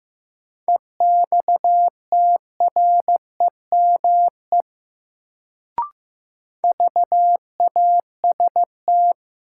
Morse code, heard as a keyed tone, about 15 wpm, 700 Hz. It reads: EXTREME VAST
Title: Sentences from Top 1300 Words - New Word Drill - 15wpm